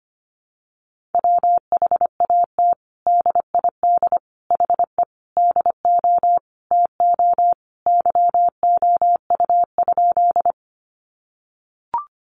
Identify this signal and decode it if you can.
Morse code, heard as a keyed tone, about 25 wpm, 700 Hz. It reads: WHAT DID HE DO TO YOU?
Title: Sentences from Top 100 Words 25wpm